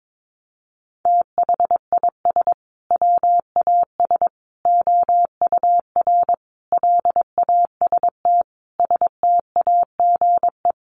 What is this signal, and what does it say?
Morse code, heard as a keyed tone, about 22 wpm, 700 Hz. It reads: THIS WAS OUR LAST STAGE
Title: Sentences from Top 1200 Words 22wpm